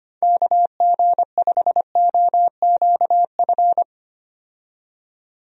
Morse code, heard as a keyed tone, about 25 wpm, 700 Hz. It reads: KG5OQF
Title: USA Call Signs - 6 Characters - 25wpm